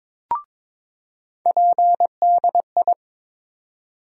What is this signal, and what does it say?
Morse code, heard as a keyed tone, about 22 wpm, 700 Hz. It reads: PDI
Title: USA Call Sign Postfixes - 3 Letters 22wpm